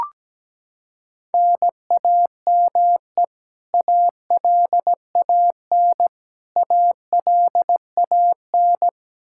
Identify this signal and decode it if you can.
Morse code, heard as a keyed tone, about 17 wpm, 700 Hz. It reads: NAME ALAN ALAN
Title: Practice QSO - Exchange 1 - Beginning - 17wpm